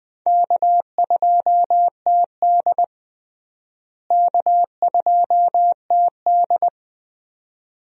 Morse code, heard as a keyed tone, about 20 wpm, 700 Hz. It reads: K2TD K2TD
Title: USA Call Signs - 4 Characters - Repeated 2x - 20wpm